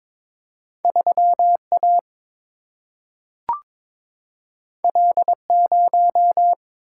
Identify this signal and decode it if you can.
Morse code, heard as a keyed tone, about 22 wpm, 700 Hz. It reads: 3 A L 0
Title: Single-Letter Single-Number 22wpm